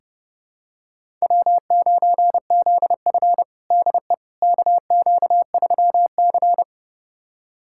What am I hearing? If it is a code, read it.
Morse code, heard as a keyed tone, about 30 wpm, 700 Hz. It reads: W9ZF DE KQ3C
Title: USA Callsign DE Callsign - 4 Characters - 30wpm